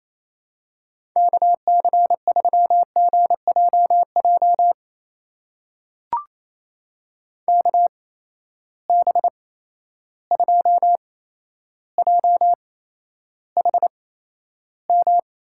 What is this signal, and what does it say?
Morse code, heard as a keyed tone, about 28 wpm, spaced as for 12 wpm, 700 Hz. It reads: KC3GJJ KB2JHM
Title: USA Call Signs - 6 Characters - 8x Character Spacing - 28wpm